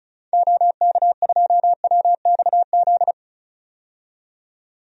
Morse code, heard as a keyed tone, about 34 wpm, 700 Hz. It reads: OK2WXZ
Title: DXCC-100 Call Signs - 6 Characters - 35wpm